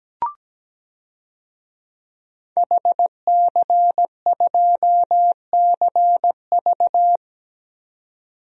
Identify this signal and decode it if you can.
Morse code, heard as a keyed tone, about 17 wpm, 700 Hz. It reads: HC2CV
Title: DXCC-100 Call Signs - 5 Characters - 17wpm